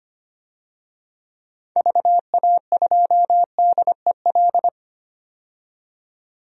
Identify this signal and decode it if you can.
Morse code, heard as a keyed tone, about 25 wpm, 700 Hz. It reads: VA2DEL
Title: DXCC-100 Call Signs - 6 Characters - 25wpm